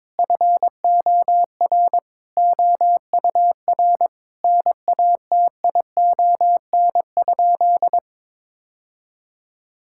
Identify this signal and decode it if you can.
Morse code, heard as a keyed tone, about 22 wpm, 700 Hz. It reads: FOR OUR NATION?
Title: Sentences from Top 1000 Words - Review Part 6 - 22wpm